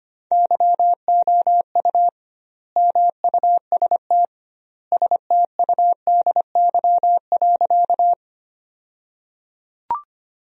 Morse code, heard as a keyed tone, about 25 wpm, 700 Hz. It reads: YOU MUST STUDY.
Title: 2x Word-Spacing - Sentences from Top 300 Words - 25wpm